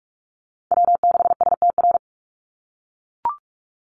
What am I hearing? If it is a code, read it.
Morse code, heard as a keyed tone, about 44 wpm, 700 Hz. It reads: W6STR
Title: Call Signs - 45wpm